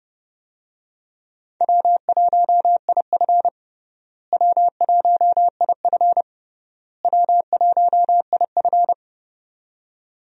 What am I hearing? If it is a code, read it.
Morse code, heard as a keyed tone, about 32 wpm, 700 Hz. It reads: W1IF W1IF W1IF
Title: USA Call Signs - 4 Characters - Repeated 3x - 30wpm